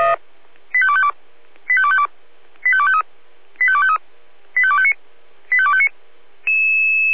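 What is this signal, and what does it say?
Five-tone selective call (ZVEI), heard as quick runs of five tones, tones of about 70 ms. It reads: 85252 85252 85253 85253 85259 85259
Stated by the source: ZVEI five-tone call 85264, received by SDR.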